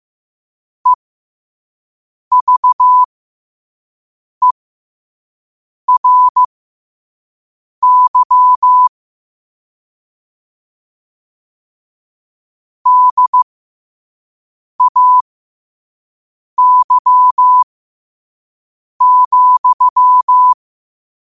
Morse code, heard as a keyed tone, about 15 wpm, spaced as for 5 wpm, 1000 Hz. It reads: EVERY DAY,
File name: qotdfiles / 5wpm